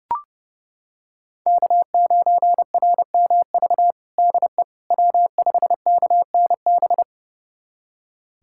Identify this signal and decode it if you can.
Morse code, heard as a keyed tone, about 32 wpm, 700 Hz. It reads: K9RMV DE W5KNB
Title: USA Callsign DE Callsign - 5 Characters - 30wpm